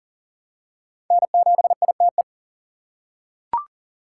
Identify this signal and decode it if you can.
Morse code, heard as a keyed tone, about 40 wpm, 700 Hz. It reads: N7ITE
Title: USA Call Signs - 3x Character Spacing - 40wpm